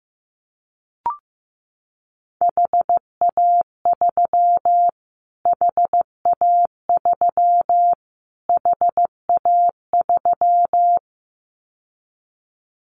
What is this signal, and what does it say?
Morse code, heard as a keyed tone, about 15 wpm, 700 Hz. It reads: HA3 HA3 HA3
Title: Callsign Prefixes for Top 25 DXCC - Repeated 3x - 15wpm